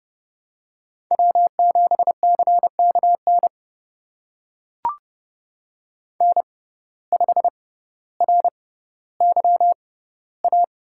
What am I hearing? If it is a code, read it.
Morse code, heard as a keyed tone, about 30 wpm, spaced as for 15 wpm, 700 Hz. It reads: W7CKN N5RYA
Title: USA Call Signs - 5 Characters - 6x Character Spacing - 30wpm